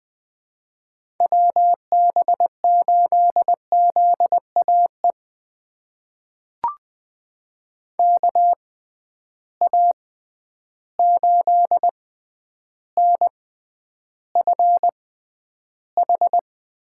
Morse code, heard as a keyed tone, about 20 wpm, spaced as for 10 wpm, 700 Hz. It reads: WB8ZAE KA8NFH
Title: USA Call Signs - 6 Characters - 6x Character Spacing - 20wpm